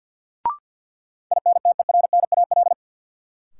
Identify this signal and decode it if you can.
Morse code, heard as a keyed tone, about 53 wpm, 700 Hz. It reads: INTERNAL
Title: Sentences from Top 1400 Words - New Word Drill - 50wpm